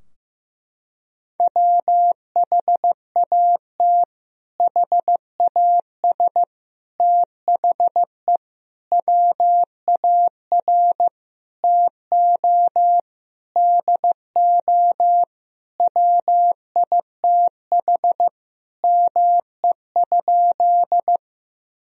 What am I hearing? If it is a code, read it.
Morse code, heard as a keyed tone, about 15 wpm, 700 Hz. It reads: WHAT HAS THE WAR TO DO WITH ME?
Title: Sentences from Top 1000 Words - Review Part 2 - 15wpm